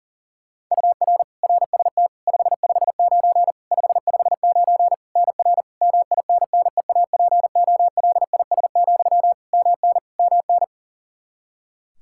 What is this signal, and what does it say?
Morse code, heard as a keyed tone, about 40 wpm, 700 Hz. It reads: UR RST 559 559 NR MINNEAPOLIS, MN MN
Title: Practice QSO - Exchange 1 - Beginning - 40wpm